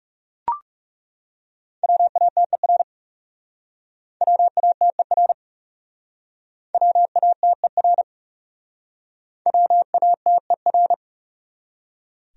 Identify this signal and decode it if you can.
Morse code, heard as a keyed tone, about 36 wpm, 700 Hz. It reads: WATER WATER WATER WATER
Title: Speed-Racing Top 500 Words 30wpm